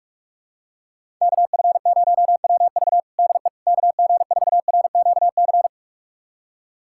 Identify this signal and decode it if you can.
Morse code, heard as a keyed tone, about 45 wpm, 700 Hz. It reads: KF0WV DE KG4RQC
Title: USA Callsign DE Callsign 45wpm